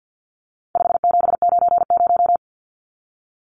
Morse code, heard as a keyed tone, about 50 wpm, 700 Hz. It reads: 5790
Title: Numbers - 4-Digit 50wpm